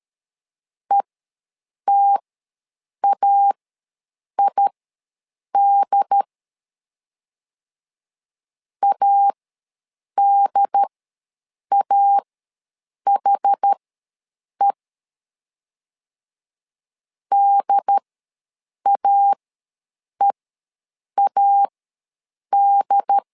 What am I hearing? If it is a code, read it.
Morse code, heard as a keyed tone, about 13 wpm, spaced as for 7 wpm, 800 Hz. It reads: ETAID ADAHE DAEAD
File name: LEZIONI / 2 UHDN / 2 UHD ws 12wpm 1722